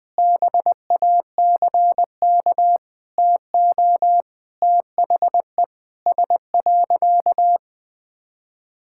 Morse code, heard as a keyed tone, about 21 wpm, 700 Hz. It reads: BACK TO THE S.
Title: Sentences with 6-Word Phrases - Spoken Beginning - 20wpm